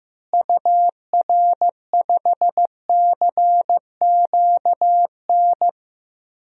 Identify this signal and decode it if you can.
Morse code, heard as a keyed tone, about 16 wpm, 700 Hz. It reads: UR5CQN
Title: DXCC-100 Call Signs - 6 Characters - Repeated 3x - 15wpm